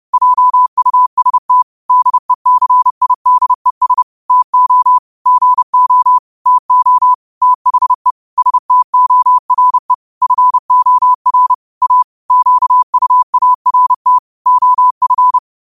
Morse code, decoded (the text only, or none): JUST DECIDES TO GO TO THE STORE FOR A QUART OF